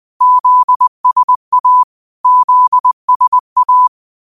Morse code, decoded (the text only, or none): ZSA ZSA